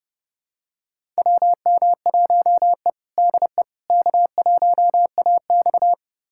Morse code, heard as a keyed tone, about 30 wpm, 700 Hz. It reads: WM1E DE K1AX
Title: USA Callsign DE Callsign - 4 Characters - 30wpm